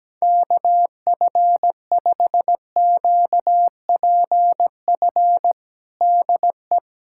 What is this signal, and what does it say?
Morse code, heard as a keyed tone, about 17 wpm, 700 Hz. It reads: KF5QPF DE
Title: USA Callsign DE Callsign 17wpm